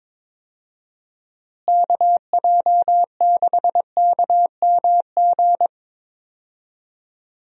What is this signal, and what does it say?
Morse code, heard as a keyed tone, about 22 wpm, 700 Hz. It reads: KJ6KMG
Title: USA Call Signs - 6 Characters - Repeated 2x - 22wpm